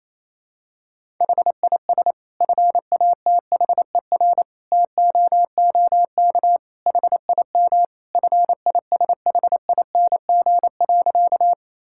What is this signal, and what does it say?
Morse code, heard as a keyed tone, about 28 wpm, 700 Hz. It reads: HIS FATHER TOOK HIM FISHING.
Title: Sentences from Top 1000 Words - Review Part 5 - 28wpm